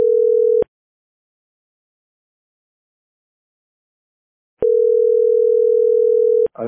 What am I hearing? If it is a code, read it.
Ringback tone (North America), 440 + 480 Hz.